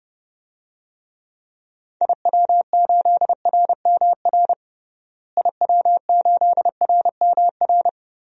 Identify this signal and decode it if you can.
Morse code, heard as a keyed tone, about 30 wpm, 700 Hz. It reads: IW8RMR IW8RMR
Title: DXCC-100 Call Signs - 6 Characters - Repeated 3x - 30wpm